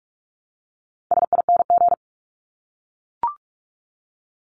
Morse code, heard as a keyed tone, about 44 wpm, 700 Hz. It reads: SING
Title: Top 500 Words 45wpm